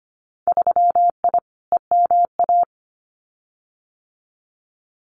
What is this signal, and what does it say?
Morse code, heard as a keyed tone, about 25 wpm, 700 Hz. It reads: 3I EMA
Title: Winter Field Day Exchanges Encore 25wpm